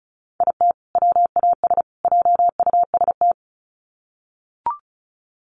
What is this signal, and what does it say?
Morse code, heard as a keyed tone, about 34 wpm, 700 Hz. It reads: IT WAS JUST
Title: Sets of 3 Words 35wpm